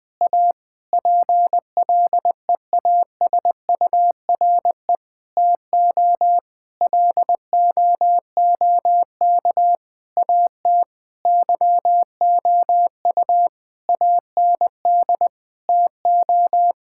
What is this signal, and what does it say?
Morse code, heard as a keyed tone, about 20 wpm, 700 Hz. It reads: A PLEASURE TO LOOK AT YOU AND TO
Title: Sentences from Top 1600 Words 20wpm